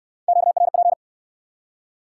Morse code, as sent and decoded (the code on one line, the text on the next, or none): --.- .-. .--.
QRP